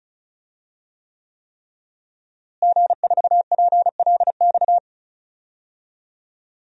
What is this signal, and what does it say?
Morse code, heard as a keyed tone, about 36 wpm, 700 Hz. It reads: G4PLX
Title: DXCC-100 Call Signs - 5 Characters - Repeated 3x - 35wpm